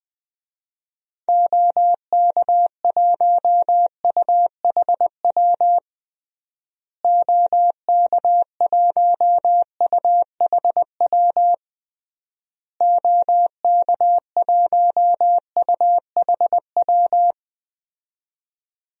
Morse code, heard as a keyed tone, about 20 wpm, 700 Hz. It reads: OK1UHW OK1UHW OK1UHW
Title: DXCC-100 Call Signs - 6 Characters - Repeated 3x - 20wpm